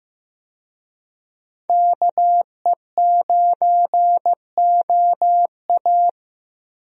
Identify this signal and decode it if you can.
Morse code, heard as a keyed tone, about 15 wpm, 700 Hz. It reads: KE9OA
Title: USA Call Signs - 5 Characters - Repeated 3x - 15wpm